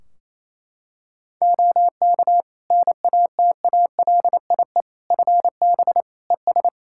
Morse code, heard as a keyed tone, about 28 wpm, 700 Hz. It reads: OK NATALIE FB ES
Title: Practice QSO - Exchange 2 - Info - 28wpm